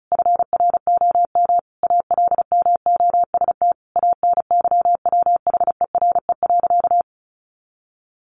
Morse code, heard as a keyed tone, about 34 wpm, 700 Hz. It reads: FROM ALMOST ANYWHERE.
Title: Sentences from Top 1400 Words 35wpm